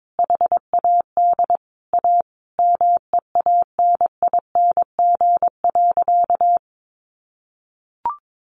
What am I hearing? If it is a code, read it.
Morse code, heard as a keyed tone, about 22 wpm, 700 Hz. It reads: HAD A MEANING.
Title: Sentences from Top 1400 Words 22wpm